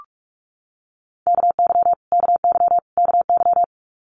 Morse code, heard as a keyed tone, about 46 wpm, 700 Hz. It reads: KY KY KY
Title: Multi-letter Phonemes Repeated 3x 45wpm